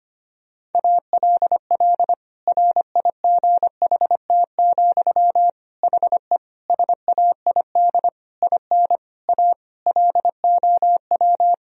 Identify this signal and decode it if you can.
Morse code, heard as a keyed tone, about 25 wpm, 700 Hz. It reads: ALL RIGHT, HE SAID IN A LOW
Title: Sentences from Top 1000 Words - Review Part 1 - 25wpm